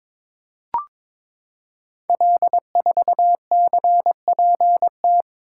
Morse code, heard as a keyed tone, about 22 wpm, 700 Hz. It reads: L4CPT
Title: Code Groups - 5 Letter:Numbers - 22wpm